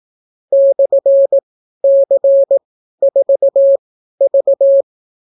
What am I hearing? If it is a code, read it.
Morse code, heard as a keyed tone, about 18 wpm, spaced as for 12 wpm, 550 Hz. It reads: /C4V